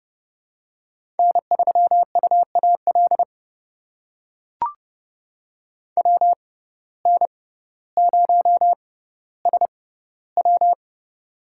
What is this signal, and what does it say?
Morse code, heard as a keyed tone, about 30 wpm, spaced as for 15 wpm, 700 Hz. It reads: N3UAL WN0SW
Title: USA Call Signs - 5 Characters - 6x Character Spacing - 30wpm